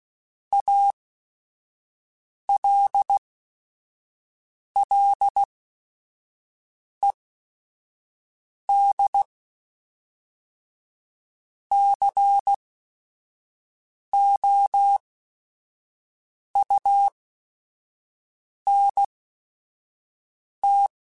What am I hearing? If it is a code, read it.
Morse code, heard as a keyed tone, about 16 wpm, spaced as for 5 wpm, 800 Hz. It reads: ALLED COUNT
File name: Technical Subjects / Technical Subjects-mp3s / XL @5